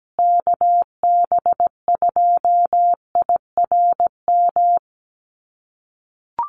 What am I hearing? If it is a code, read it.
Morse code, heard as a keyed tone, about 17 wpm, 700 Hz. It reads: KB2IRM
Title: USA Call Signs Repeated 2x 17wpm